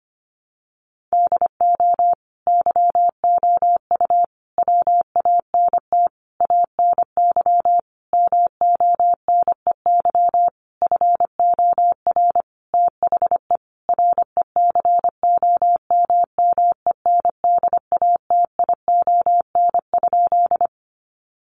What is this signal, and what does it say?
Morse code, heard as a keyed tone, about 25 wpm, 700 Hz. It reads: DO YOU WANT ANY MONEY FOR THE RECOMMENDATION?
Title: Sentences from Top 1400 Words 25wpm